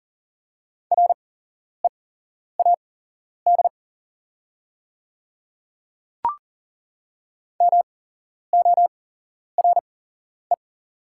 Morse code, heard as a keyed tone, about 41 wpm, spaced as for 11 wpm, 700 Hz. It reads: READ MORE
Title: 8x Character Spacing - Top 300 Words - 40wpm